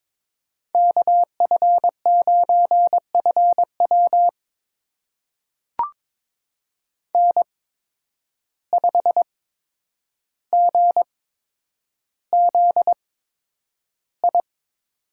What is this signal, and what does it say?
Morse code, heard as a keyed tone, about 22 wpm, spaced as for 9 wpm, 700 Hz. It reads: KF9FW N5GZI
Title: USA Call Signs - 5 Characters - 8x Character Spacing - 22wpm